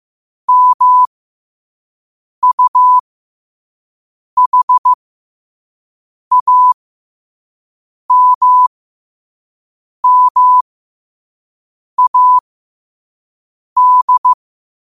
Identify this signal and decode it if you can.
Morse code, heard as a keyed tone, about 15 wpm, spaced as for 5 wpm, 1000 Hz. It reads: MUHAMMAD
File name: qotdfiles / 5wpm